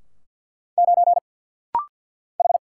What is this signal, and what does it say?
Morse code, heard as a keyed tone, about 48 wpm, 700 Hz. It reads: 9 H
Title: Single Letter-Number - Rapid-Fire 50wpm